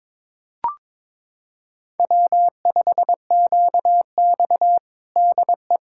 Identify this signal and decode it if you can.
Morse code, heard as a keyed tone, about 22 wpm, 700 Hz. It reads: W5QX DE
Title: USA Callsign DE Callsign - 4 Characters - 22wpm